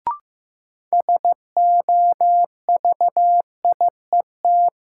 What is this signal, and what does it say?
Morse code, heard as a keyed tone, about 15 wpm, 700 Hz. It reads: SOVIET
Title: Sets of 5 Words Encore - New Word Drill - 15wpm